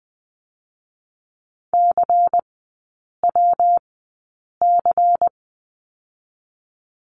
Morse code, heard as a keyed tone, about 20 wpm, 700 Hz. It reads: C W C
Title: 114 Lesson 19e - Letter w - Three Characters 20wpm